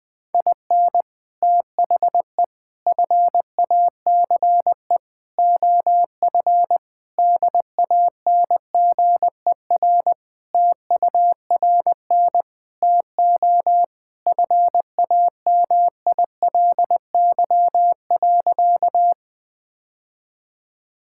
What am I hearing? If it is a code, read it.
Morse code, heard as a keyed tone, about 20 wpm, 700 Hz. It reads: IN THE FACE OF DANGER TURN TO FAMILY.
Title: Sentences with 3-Word Phrases - Spoken Beginning - 20wpm